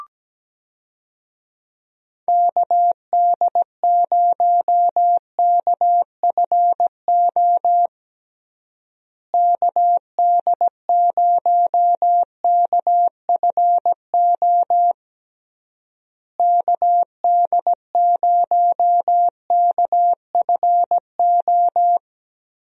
Morse code, heard as a keyed tone, about 17 wpm, 700 Hz. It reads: KD0KFO KD0KFO KD0KFO
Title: USA Call Signs - 6 Characters - Repeated 3x - 17wpm